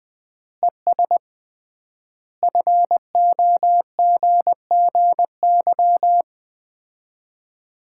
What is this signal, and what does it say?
Morse code, heard as a keyed tone, about 20 wpm, 700 Hz. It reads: ES FOGGY
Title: Common QSO Phrases - 3x Word-Spacing - 20wpm